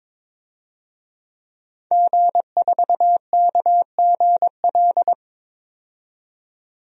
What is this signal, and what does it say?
Morse code, heard as a keyed tone, about 22 wpm, 700 Hz. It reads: G4KGL
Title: DXCC-100 Call Signs - 5 Characters - 22wpm